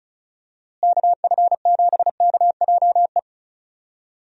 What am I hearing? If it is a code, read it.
Morse code, heard as a keyed tone, about 34 wpm, 700 Hz. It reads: KF7KJE
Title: USA Call Signs - 6 Characters - Repeated 2x - 35wpm